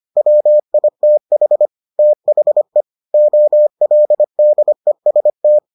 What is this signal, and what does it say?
Morse code, heard as a keyed tone, about 25 wpm, 600 Hz. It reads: WITH THE OLDEST